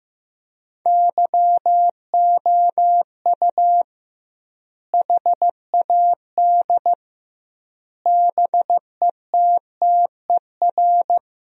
Morse code, heard as a keyed tone, about 15 wpm, 700 Hz. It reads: YOU HAD BETTER
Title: 2x Word-Spacing - Sentences from Top 400 Words - 15wpm